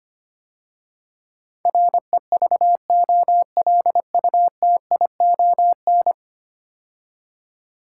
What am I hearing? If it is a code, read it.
Morse code, heard as a keyed tone, about 25 wpm, 700 Hz. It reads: REVOLUTION
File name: 10 Letter Words - First 3 Letters Given - 25wpm